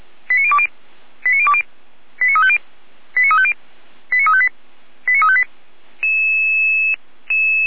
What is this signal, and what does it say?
Five-tone selective call (ZVEI), heard as quick runs of five tones, tones of about 80 ms. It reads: 89020 89020 89350 89350 89358 89358
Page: /455/SDR/ZVEI/89590